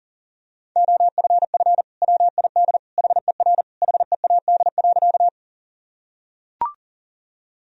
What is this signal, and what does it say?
Morse code, heard as a keyed tone, about 40 wpm, 700 Hz. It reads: OFF WID HER HEAD.
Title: Sentences from Top 300 Words and Abbreviations 40wpm